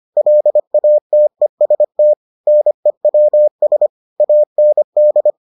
Morse code, heard as a keyed tone, about 25 wpm, 600 Hz. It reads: LATEST NEWS AND